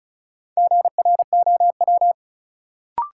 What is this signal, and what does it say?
Morse code, heard as a keyed tone, about 35 wpm, 700 Hz. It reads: GROW
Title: Sentences from Top 300 Words - New Word Drill - 35wpm